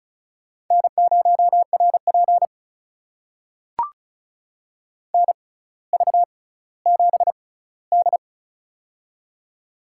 Morse code, heard as a keyed tone, about 36 wpm, spaced as for 18 wpm, 700 Hz. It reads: N0RP NV7D
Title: USA Call Signs - 4 Characters - 6x Character Spacing - 35wpm